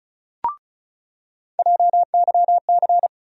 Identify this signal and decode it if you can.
Morse code, heard as a keyed tone, about 36 wpm, 700 Hz. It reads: JYC
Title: USA Call Sign Postfixes - 3 Letters 35wpm